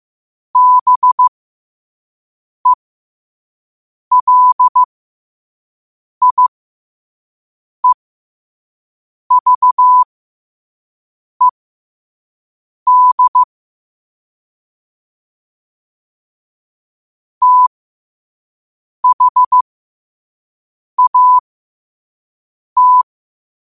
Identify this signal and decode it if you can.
Morse code, heard as a keyed tone, about 15 wpm, spaced as for 5 wpm, 1000 Hz. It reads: BELIEVED THAT
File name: qotdfiles / 5wpm